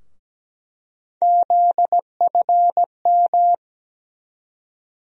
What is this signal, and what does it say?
Morse code, heard as a keyed tone, about 17 wpm, 700 Hz. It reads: ZFM
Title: USA Call Sign Postfixes - 3 Letters 17wpm